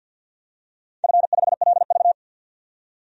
Morse code, heard as a keyed tone, about 53 wpm, 700 Hz. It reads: F5LV
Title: DXCC-100 Call Signs - 4 Characters - 50wpm